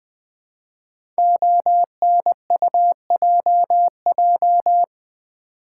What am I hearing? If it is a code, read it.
Morse code, heard as a keyed tone, about 20 wpm, 700 Hz. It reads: ONUJJ
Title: Code Groups - 5 Letters - 20wpm